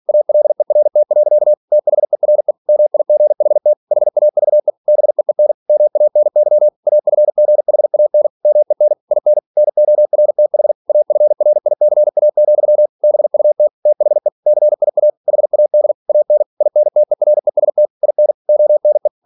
Morse code, heard as a keyed tone, about 46 wpm, 600 Hz. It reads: ALERT. THERE MIGHT HAVE BEEN MANY AFGHAN MEN IN NORTH AFRICA, BUT THE CIA HAD AN INTEREST IN ONE
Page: Never 47wpm